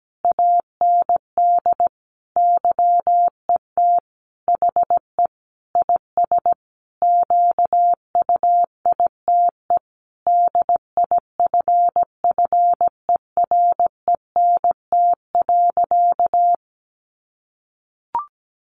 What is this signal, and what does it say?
Morse code, heard as a keyed tone, about 17 wpm, 700 Hz. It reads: AND YET HE IS QUITE DIFFERENT.
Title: Sentences from Top 700 Words 17wpm